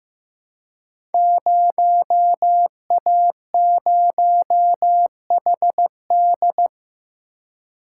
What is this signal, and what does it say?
Morse code, heard as a keyed tone, about 15 wpm, 700 Hz. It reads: 0A0HD
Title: Code Groups - 5 Letter:Numbers - 15wpm